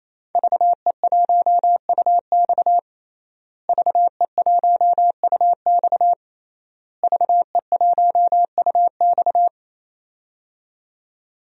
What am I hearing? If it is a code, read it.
Morse code, heard as a keyed tone, about 28 wpm, 700 Hz. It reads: VE1UX VE1UX VE1UX
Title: DXCC-100 Call Signs - 5 Characters - Repeated 3x - 28wpm